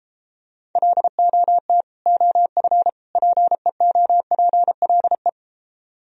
Morse code, heard as a keyed tone, about 32 wpm, 700 Hz. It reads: LOT OF PEOPLE